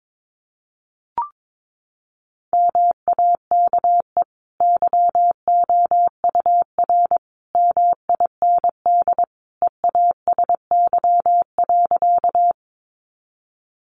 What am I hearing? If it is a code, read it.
Morse code, heard as a keyed tone, about 22 wpm, 700 Hz. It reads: MAKE YOUR MIND EASY.
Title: Sentences from Top 700 Words 22wpm